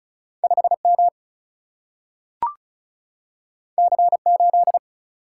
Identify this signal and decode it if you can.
Morse code, heard as a keyed tone, about 36 wpm, 700 Hz. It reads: 5 M C 8
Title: Single-Letter Single-Number 35wpm